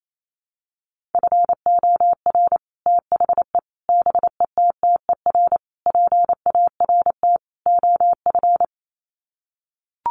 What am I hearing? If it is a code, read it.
Morse code, heard as a keyed tone, about 28 wpm, 700 Hz. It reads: FOR THE BETTER PART OF
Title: Sets of 5 Words Encore2 28wpm